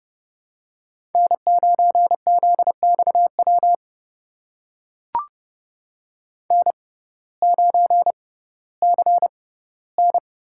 Morse code, heard as a keyed tone, about 30 wpm, spaced as for 17 wpm, 700 Hz. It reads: N9ZXW N9CN
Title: USA Call Signs - 5 Characters - 6x Character Spacing - 30wpm